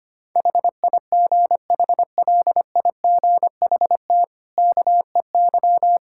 Morse code, heard as a keyed tone, about 25 wpm, 700 Hz. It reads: HIGHLIGHT KEY
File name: Easy Sentences with Hints - 25wpm